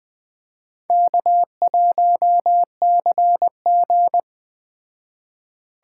Morse code, heard as a keyed tone, about 20 wpm, 700 Hz. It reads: K1CG
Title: Speed Racing - USA 1x2 Call Signs 17wpm